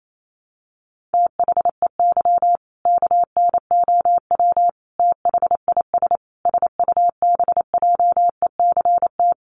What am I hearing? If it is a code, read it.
Morse code, heard as a keyed tone, about 28 wpm, 700 Hz. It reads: THEY KNOW THIS SUBJECT
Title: Sentences from Top 800 Words 28wpm